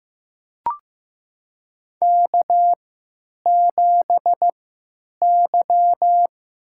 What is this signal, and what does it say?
Morse code, heard as a keyed tone, about 15 wpm, spaced as for 9 wpm, 700 Hz. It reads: K7Y
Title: USA Call Signs - 3x Character Spacing - 15wpm